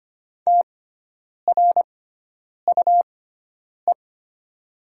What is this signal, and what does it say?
Morse code, heard as a keyed tone, about 25 wpm, spaced as for 9 wpm, 700 Hz. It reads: TRUE
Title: 6x Character Spacing - Top 500 Words - 25wpm